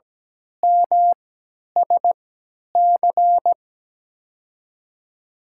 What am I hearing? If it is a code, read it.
Morse code, heard as a keyed tone, about 17 wpm, spaced as for 10 wpm, 700 Hz. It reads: MSC
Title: USA Call Signs - 5 Characters - 3x Character Spacing - 17wpm